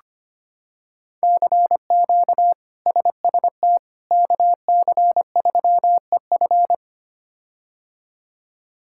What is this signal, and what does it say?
Morse code, heard as a keyed tone, about 25 wpm, 700 Hz. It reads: CQ SST KC3EF
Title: K1USN SST QSO Practice - 25wpm